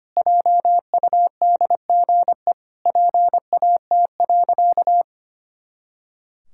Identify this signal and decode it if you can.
Morse code, heard as a keyed tone, about 25 wpm, 700 Hz. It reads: JUDGE PAT.
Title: Pangram Challenge v2 - 25wpm